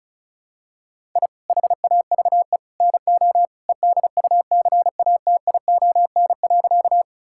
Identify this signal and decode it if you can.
Morse code, heard as a keyed tone, about 36 wpm, 700 Hz. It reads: I HAVE NO EDUCATION.
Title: Sentences from Top 1000 Words - Review Part 6 - 35wpm